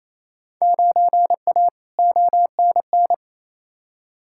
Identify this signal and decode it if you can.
Morse code, heard as a keyed tone, about 28 wpm, 700 Hz. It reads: 9A ONN
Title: Field Day Exchanges Encore - 28wpm